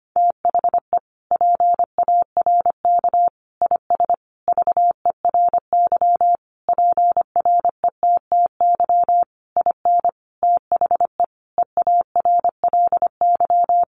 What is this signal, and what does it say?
Morse code, heard as a keyed tone, about 25 wpm, 700 Hz. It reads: THE PARK IS VERY PRETTY IN THE EARLY
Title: Sentences from Top 1300 Words 25wpm